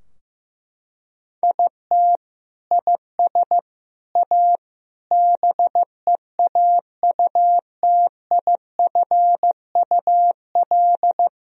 Morse code, heard as a keyed tone, about 15 wpm, 700 Hz. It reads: IT IS A BEAUTIFUL
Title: Sentences from Top 1000 Words - Review Part 2 - 15wpm